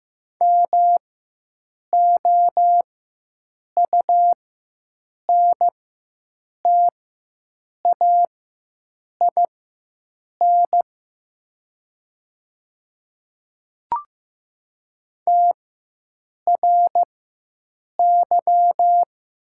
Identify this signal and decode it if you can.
Morse code, heard as a keyed tone, about 15 wpm, spaced as for 7 wpm, 700 Hz. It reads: MOUNTAIN TRY
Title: 4x Character Spacing - Top 300 Words - 15wpm